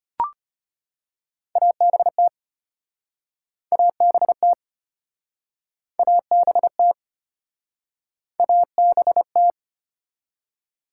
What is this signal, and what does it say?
Morse code, heard as a keyed tone, about 33 wpm, 700 Hz. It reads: ABT ABT ABT ABT